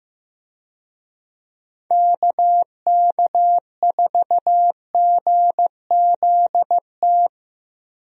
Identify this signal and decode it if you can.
Morse code, heard as a keyed tone, about 15 wpm, 700 Hz. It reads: KK4GZT
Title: USA Call Signs - 6 Characters - Repeated 3x - 15wpm